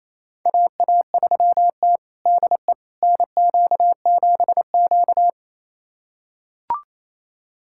Morse code, heard as a keyed tone, about 27 wpm, 700 Hz. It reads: AA3T DE NQ7Q
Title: USA Callsign DE Callsign - 4 Characters - 28wpm